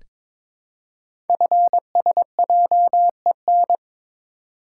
Morse code, heard as a keyed tone, about 22 wpm, 700 Hz. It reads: FSJEN
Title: Code Groups - 5 Letters - 22wpm